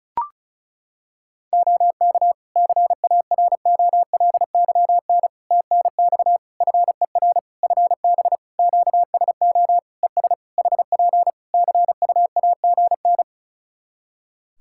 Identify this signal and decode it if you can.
Morse code, heard as a keyed tone, about 34 wpm, 700 Hz. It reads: OK CAROLYN TNX FER FB QSO ES HP CUAGN
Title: Practice QSO - Exchange 4 - Ending - 35wpm